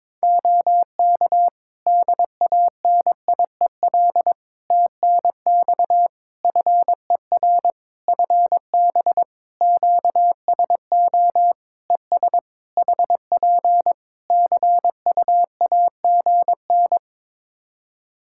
Morse code, heard as a keyed tone, about 22 wpm, 700 Hz. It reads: OK DANIEL TNX FER FB QSO ES HP CUAGN
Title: Practice QSO - Exchange 4 - Ending - 22wpm